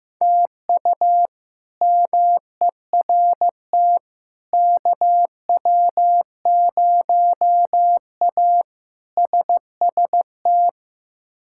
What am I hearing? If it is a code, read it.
Morse code, heard as a keyed tone, about 15 wpm, 700 Hz. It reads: TU MERT KW0A SST
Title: K1USN SST QSO Practice - 15wpm